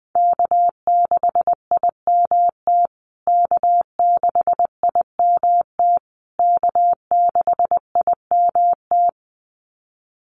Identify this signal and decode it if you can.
Morse code, heard as a keyed tone, about 20 wpm, 700 Hz. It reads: K6IMT K6IMT K6IMT
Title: USA Call Signs Repeated 3x 20wpm